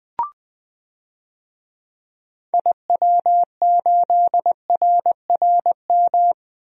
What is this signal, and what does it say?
Morse code, heard as a keyed tone, about 20 wpm, 700 Hz. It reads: IW8RRM
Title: DXCC-100 Call Signs - 6 Characters - 20wpm